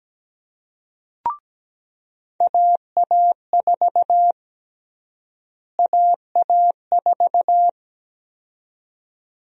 Morse code, heard as a keyed tone, about 17 wpm, 700 Hz. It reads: AA4 AA4
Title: USA Call Sign Prefixes - Repeated 2x - 17wpm